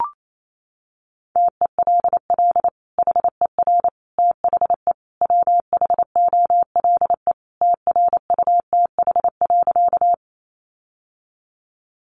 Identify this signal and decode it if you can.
Morse code, heard as a keyed tone, about 28 wpm, 700 Hz. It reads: TELL HER THE WHOLE TRUTH.
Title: Sentences from Top 800 Words 28wpm